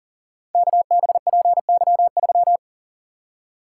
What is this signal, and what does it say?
Morse code, heard as a keyed tone, about 40 wpm, 700 Hz. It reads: KBPY3
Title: Code Groups - 5 Letter:Numbers - 40wpm